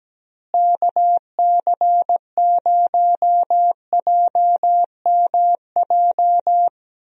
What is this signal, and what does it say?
Morse code, heard as a keyed tone, about 17 wpm, 700 Hz. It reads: KC0JMJ
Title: USA Call Signs - 6 Characters - Repeated 3x - 17wpm